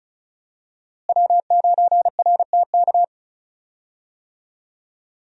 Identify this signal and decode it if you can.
Morse code, heard as a keyed tone, about 34 wpm, 700 Hz. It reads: W9RTK
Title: USA Call Signs - 5 Characters - Repeated 2x - 35wpm